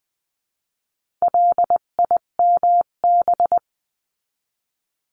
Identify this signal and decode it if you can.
Morse code, heard as a keyed tone, about 20 wpm, 700 Hz. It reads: LIMB
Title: 258 Lesson 42c - Words 16-20wpm